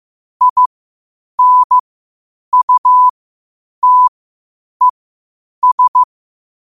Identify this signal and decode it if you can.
Morse code, heard as a keyed tone, about 15 wpm, spaced as for 8 wpm, 1000 Hz. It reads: INUTES
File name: 8wpm